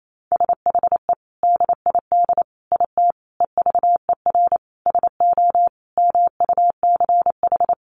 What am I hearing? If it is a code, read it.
Morse code, heard as a keyed tone, about 28 wpm, 700 Hz. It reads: SHE DID IT EVER SO MUCH
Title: Sentences from Top 400 Words 28wpm